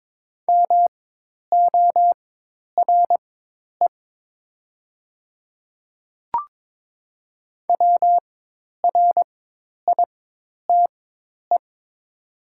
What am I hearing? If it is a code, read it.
Morse code, heard as a keyed tone, about 22 wpm, spaced as for 10 wpm, 700 Hz. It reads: MORE WRITE
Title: 4x Character Spacing - Top 100 Words - 22wpm